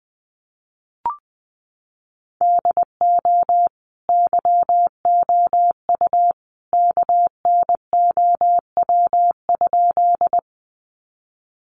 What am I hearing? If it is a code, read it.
Morse code, heard as a keyed tone, about 20 wpm, 700 Hz. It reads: DO YOU KNOW?
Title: Sentences from Top 100 Words 20wpm